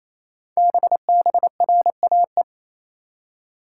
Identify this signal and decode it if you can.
Morse code, heard as a keyed tone, about 27 wpm, 700 Hz. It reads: BBRAE
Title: Code Groups - 5 Letters - 28wpm